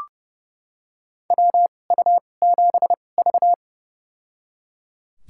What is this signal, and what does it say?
Morse code, heard as a keyed tone, about 30 wpm, spaced as for 22 wpm, 700 Hz. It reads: WU7V
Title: USA Call Signs - 4 Characters - 2x Character Spacing - 30wpm